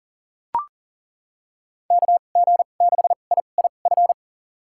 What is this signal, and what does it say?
Morse code, heard as a keyed tone, about 40 wpm, spaced as for 29 wpm, 700 Hz. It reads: KG6IIF
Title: USA Call Signs - 6 Characters - 2x Character Spacing - 40wpm